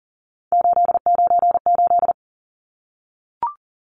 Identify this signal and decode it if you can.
Morse code, heard as a keyed tone, about 40 wpm, 700 Hz. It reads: 898
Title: Numbers - 3-Digit 40wpm